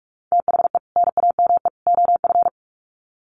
Morse code, heard as a keyed tone, about 44 wpm, 700 Hz. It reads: THE NAME OF
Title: Sets of 3 Words 45wpm